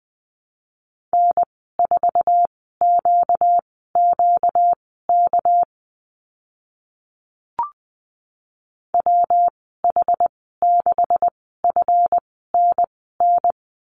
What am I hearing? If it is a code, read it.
Morse code, heard as a keyed tone, about 20 wpm, spaced as for 14 wpm, 700 Hz. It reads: N4QQK WH6FNN
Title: 238 Lesson 38g - Letter q - Call Signs 20wpm